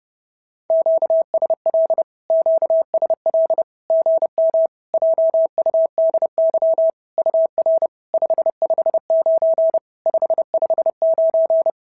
Morse code, heard as a keyed tone, about 30 wpm, 650 Hz. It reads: QSL QSL GM JUDY UR 559 559
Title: SOTA QSO Practice 30wpm